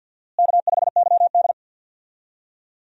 Morse code, heard as a keyed tone, about 53 wpm, 700 Hz. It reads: K5YD
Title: USA Call Signs - 4 Characters - Repeated 3x - 50wpm